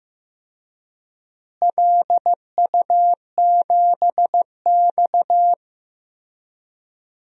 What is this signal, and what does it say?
Morse code, heard as a keyed tone, about 15 wpm, 700 Hz. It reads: LU7X
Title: DXCC-100 Call Signs - 4 Characters - Repeated 3x - 15wpm